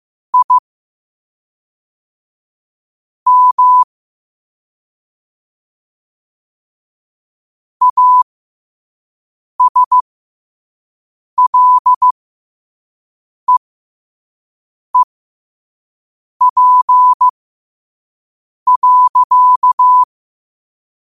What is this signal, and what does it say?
Morse code, heard as a keyed tone, about 15 wpm, spaced as for 5 wpm, 1000 Hz. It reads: I M ASLEEP.
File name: qotdfiles / 5wpm